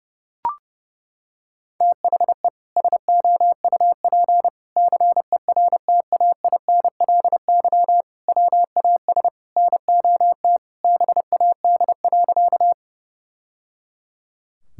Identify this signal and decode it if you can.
Morse code, heard as a keyed tone, about 30 wpm, 700 Hz. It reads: THE SOUP CERTAINLY WAS NOT BAD.
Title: Sentences from Top 1000 Words - Review Part 6 - 30wpm